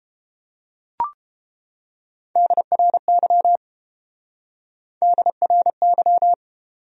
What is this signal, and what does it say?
Morse code, heard as a keyed tone, about 29 wpm, 700 Hz. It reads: DRY DRY